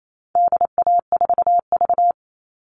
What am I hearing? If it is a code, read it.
Morse code, heard as a keyed tone, about 28 wpm, 700 Hz. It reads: DA4V
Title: All Cut Numbers - 4-Digit 28wpm